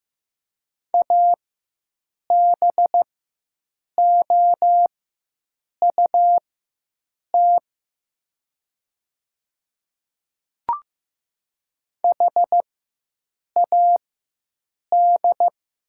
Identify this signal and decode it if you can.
Morse code, heard as a keyed tone, about 15 wpm, spaced as for 7 wpm, 700 Hz. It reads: ABOUT HAD
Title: 4x Character Spacing - Top 300 Words - 15wpm